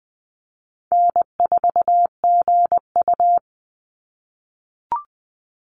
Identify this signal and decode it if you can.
Morse code, heard as a keyed tone, about 19 wpm, 700 Hz. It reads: N4GU
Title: Call Signs - 20wpm